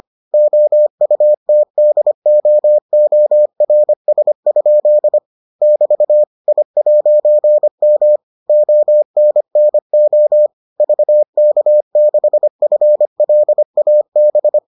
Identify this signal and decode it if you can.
Morse code, heard as a keyed tone, about 25 wpm, 600 Hz. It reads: OUTDOORS? <BT> I'M ONNO VK6FLAB